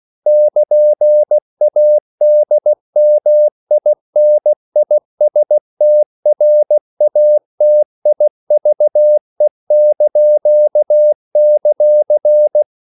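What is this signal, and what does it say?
Morse code, heard as a keyed tone, about 16 wpm, 600 Hz. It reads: <KN>ADMINISTRATIVE);
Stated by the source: CW practice of national capitals at 16 words per minute, from ON6ZQ.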